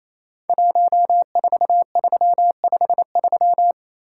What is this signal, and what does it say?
Morse code, heard as a keyed tone, about 27 wpm, 690 Hz. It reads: 14353
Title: Code Groups - 5 Letter:Numbers - 28wpm